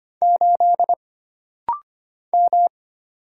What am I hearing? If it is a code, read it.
Morse code, heard as a keyed tone, about 25 wpm, 700 Hz. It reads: 8 M
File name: Single Character (Letter:Number:Punctuation:Prosign) - Rapid-Fire - 25wpm